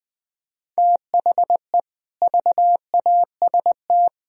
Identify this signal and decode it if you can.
Morse code, heard as a keyed tone, about 21 wpm, 700 Hz. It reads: THE VAST